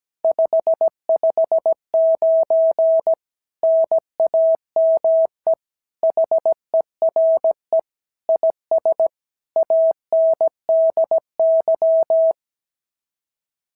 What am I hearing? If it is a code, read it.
Morse code, heard as a keyed tone, about 16 wpm, 650 Hz. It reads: 559 NAME HERE IS ANDY
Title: SOTA QSO Practice 17wpm